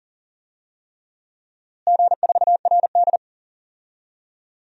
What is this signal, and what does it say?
Morse code, heard as a keyed tone, about 40 wpm, 700 Hz. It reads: G4RD
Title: DXCC-100 Call Signs - 4 Characters - Repeated 3x - 40wpm